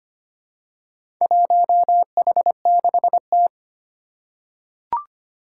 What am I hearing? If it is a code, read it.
Morse code, heard as a keyed tone, about 25 wpm, 700 Hz. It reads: 1H6T
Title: Four Challenging Letter-Number Combinations 25wpm